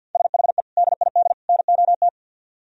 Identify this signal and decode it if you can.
Morse code, heard as a keyed tone, about 61 wpm, 700 Hz. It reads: SHE DID NOT
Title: Speed-Racing - Sets of 3 Words 50wpm